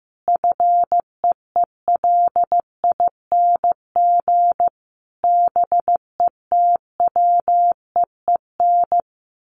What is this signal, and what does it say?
Morse code, heard as a keyed tone, about 15 wpm, 700 Hz. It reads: FEELING BETWEEN
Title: Sentences from Top 1400 Words 15wpm